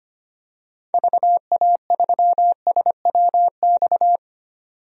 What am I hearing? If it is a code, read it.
Morse code, heard as a keyed tone, about 25 wpm, 700 Hz. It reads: VA3SWX
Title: DXCC-100 Call Signs - 6 Characters - 25wpm